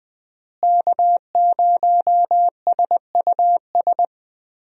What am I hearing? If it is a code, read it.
Morse code, heard as a keyed tone, about 20 wpm, 700 Hz. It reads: K0SUS
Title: USA Call Signs - 5 Characters - Repeated 2x - 20wpm